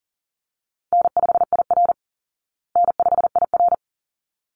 Morse code, heard as a keyed tone, about 40 wpm, 700 Hz. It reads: N5IR N5IR
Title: Call Signs Repeated 3x - 40wpm